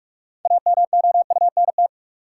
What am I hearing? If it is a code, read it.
Morse code, heard as a keyed tone, about 48 wpm, 700 Hz. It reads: AMOUNT
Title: Sentences from Top 700 Words - New Word Drill - 45wpm